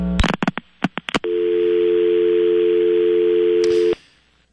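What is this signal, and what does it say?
Dial tone (North America), 350 + 440 Hz.
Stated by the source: Heard on the KTCK airwaves in Gordo and Kevin Turner's Mavs Pregame / Rant Revival Show.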